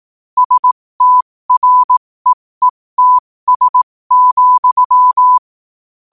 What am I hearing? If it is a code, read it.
Morse code, heard as a keyed tone, about 18 wpm, spaced as for 15 wpm, 1000 Hz. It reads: STREETS,
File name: qotdfiles / 15wpm